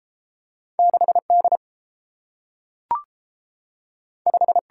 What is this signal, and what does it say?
Morse code, heard as a keyed tone, about 30 wpm, 700 Hz. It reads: 6 D 5